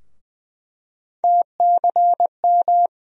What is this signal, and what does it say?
Morse code, heard as a keyed tone, about 20 wpm, 700 Hz. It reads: TCM
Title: USA Call Sign Postfixes - 3 Letters 20wpm